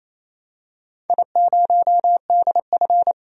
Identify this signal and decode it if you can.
Morse code, heard as a keyed tone, about 28 wpm, 700 Hz. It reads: I0DF
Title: DXCC-100 Call Signs - 4 Characters - 28wpm